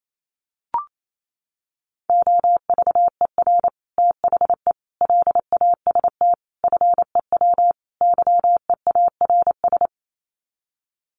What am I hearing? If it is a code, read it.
Morse code, heard as a keyed tone, about 28 wpm, 700 Hz. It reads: OVER THE LAST FEW YEARS
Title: Sets of 5 Words Encore 28wpm